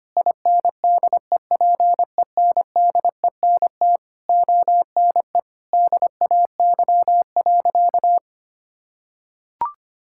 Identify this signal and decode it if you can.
Morse code, heard as a keyed tone, about 25 wpm, 700 Hz. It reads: INDEPENDENT ONE DAY.
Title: Sentences from Top 1000 Words - Review Part 6 - 25wpm